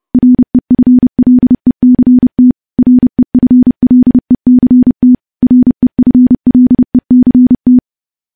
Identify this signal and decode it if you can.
Morse code, heard as a keyed tone, about 32 wpm, 250 Hz. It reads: REFLECT REFLECT REFLECT
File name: morse / morse-training-30WPM